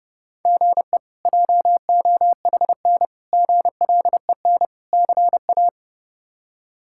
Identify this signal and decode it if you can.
Morse code, heard as a keyed tone, about 32 wpm, 700 Hz. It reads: GE JOHN GLEN CA
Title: K1USN SST QSO Practice - 30wpm